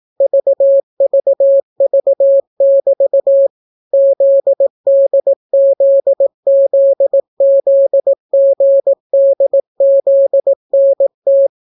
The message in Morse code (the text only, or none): VVV<BT> ZDZZZGDZNT